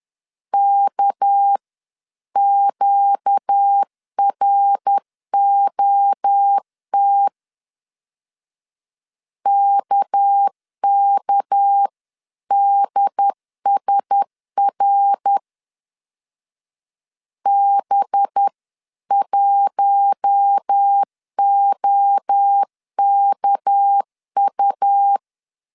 Morse code, heard as a keyed tone, about 10 wpm, 800 Hz. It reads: KQROT KKDSR B1OKU
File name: LEZIONI / 6 PKX0 / 6 PK 10wpm 2436